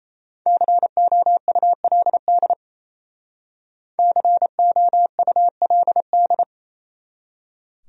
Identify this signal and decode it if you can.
Morse code, heard as a keyed tone, about 37 wpm, 700 Hz. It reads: COULD COULD